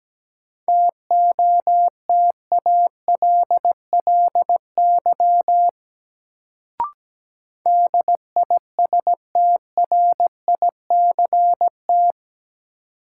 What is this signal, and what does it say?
Morse code, heard as a keyed tone, about 17 wpm, 700 Hz. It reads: TOTALLY DISTRICT
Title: Sentences from Top 1600 Words - New Word Drill Alt - 17wpm